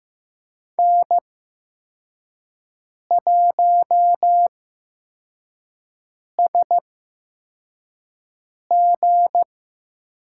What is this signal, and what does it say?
Morse code, heard as a keyed tone, about 15 wpm, 700 Hz. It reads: N 1 S G
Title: USA Call Signs - 5 Characters - 8x Character Spacing - 15wpm